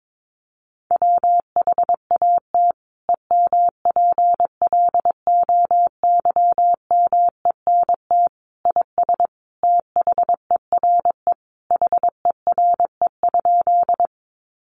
Morse code, heard as a keyed tone, about 22 wpm, 700 Hz. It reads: WHAT EMPLOYMENT IS THERE HERE?
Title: Sentences from Top 1500 Words 22wpm